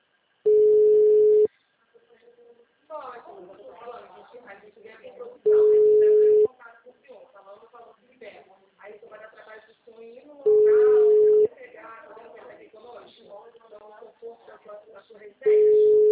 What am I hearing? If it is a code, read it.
Ringback tone (Europe, 425 Hz), with speech.